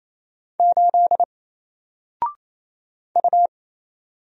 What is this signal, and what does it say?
Morse code, heard as a keyed tone, about 28 wpm, 700 Hz. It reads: 8 U
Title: Single Letter-Number 28wpm